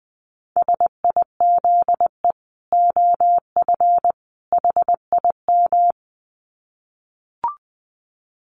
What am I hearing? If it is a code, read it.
Morse code, heard as a keyed tone, about 20 wpm, 700 Hz. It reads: SIZE OF HIM
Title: Dit Challenge 20wpm